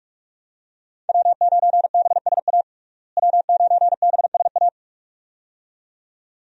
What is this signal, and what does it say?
Morse code, heard as a keyed tone, about 41 wpm, 700 Hz. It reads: W9BSA W9BSA
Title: USA Call Signs - 5 Characters - Repeated 2x - 45wpm